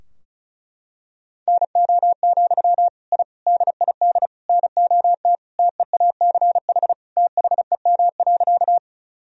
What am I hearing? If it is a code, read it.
Morse code, heard as a keyed tone, about 34 wpm, 700 Hz. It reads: NO, I DID NOT TEACH THEM.
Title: Sentences from Top 1000 Words - Review Part 2 - 35wpm